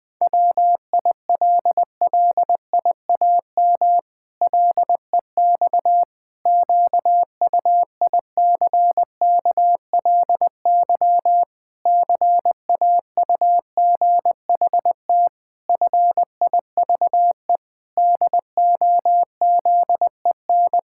Morse code, heard as a keyed tone, about 20 wpm, 700 Hz. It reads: WILLIAM LEX QUICKLY CAUGHT FIVE DOZEN
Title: Pangram Challenge v2 - 20wpm